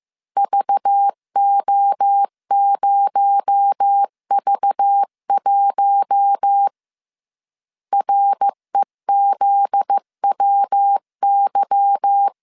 Morse code, heard as a keyed tone, about 15 wpm, 800 Hz. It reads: VO0V1 REZWY
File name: CW / DET / 1 9 1500C S C 14wpm 3245